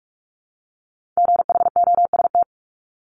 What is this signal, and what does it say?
Morse code, heard as a keyed tone, about 46 wpm, 700 Hz. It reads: GHOST
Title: Multi-letter Phonemes 45wpm